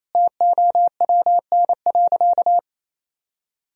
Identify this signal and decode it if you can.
Morse code, heard as a keyed tone, about 28 wpm, 700 Hz. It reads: TOWN.
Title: Sentences from Top 1000 Words - Review Part 2 - 28wpm